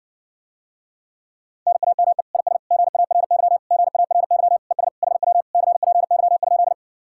Morse code, heard as a keyed tone, about 69 wpm, 700 Hz. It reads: NAME IS GARY GARY ES HW CPY?